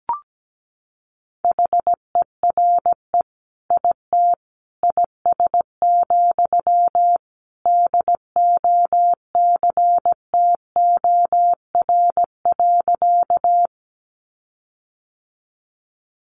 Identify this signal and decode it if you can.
Morse code, heard as a keyed tone, about 17 wpm, 700 Hz. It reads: HERE IT IS, DOCTOR.
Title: Sentences from Top 1500 Words 17wpm